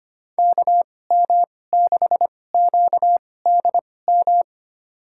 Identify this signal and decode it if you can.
Morse code, heard as a keyed tone, about 25 wpm, spaced as for 18 wpm, 700 Hz. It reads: KM6QDM
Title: USA Call Signs - 2x Character Spacing - 25wpm